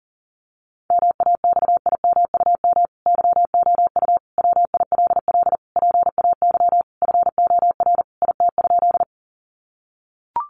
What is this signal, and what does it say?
Morse code, heard as a keyed tone, about 40 wpm, 700 Hz. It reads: MAXIMUM YOU WILL PAY FOR IT?
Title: Sentences from Top 1300 Words 40wpm